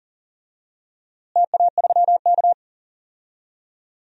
Morse code, heard as a keyed tone, about 40 wpm, 700 Hz. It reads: TA3K
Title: DXCC-100 Call Signs - 4 Characters - Repeated 3x - 40wpm